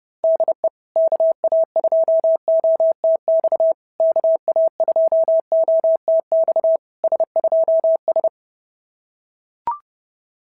Morse code, heard as a keyed tone, about 28 wpm, 660 Hz. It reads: DE KA2OTX KA2OTX S2S
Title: SOTA QSO Practice 30wpm